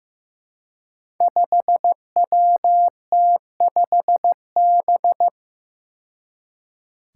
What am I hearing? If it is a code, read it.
Morse code, heard as a keyed tone, about 15 wpm, 700 Hz. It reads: 5WT5B
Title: Five Challenging Letter-Number Combinations 15wpm